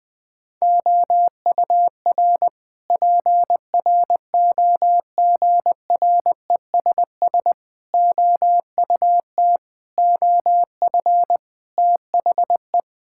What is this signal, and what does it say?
Morse code, heard as a keyed tone, about 20 wpm, 700 Hz. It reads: OUR PROGRESS OUT OF THE
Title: Sentences from Top 1600 Words 20wpm